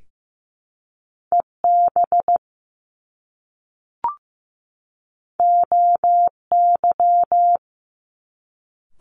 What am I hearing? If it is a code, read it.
Morse code, heard as a keyed tone, about 15 wpm, 700 Hz. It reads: EB OY
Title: Top Two Letter Combinations in Words 15wpm